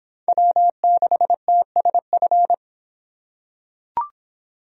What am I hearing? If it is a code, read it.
Morse code, heard as a keyed tone, about 24 wpm, 700 Hz. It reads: W6TSF